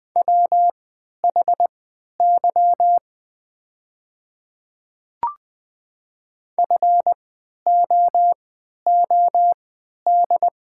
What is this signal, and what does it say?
Morse code, heard as a keyed tone, about 20 wpm, spaced as for 11 wpm, 700 Hz. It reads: WHY FOOD
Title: 3x Character Spacing - Top 300 Words - 20wpm